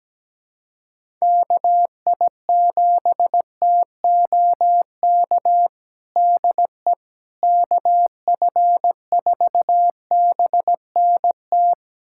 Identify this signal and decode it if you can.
Morse code, heard as a keyed tone, about 17 wpm, 700 Hz. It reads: KI7TOK DE KF4BNT
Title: USA Callsign DE Callsign - 6 Characters - 17wpm